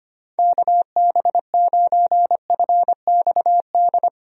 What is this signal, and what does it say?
Morse code, heard as a keyed tone, about 25 wpm, 700 Hz. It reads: KB9FXD
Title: USA Call Signs - 6 Characters - 25wpm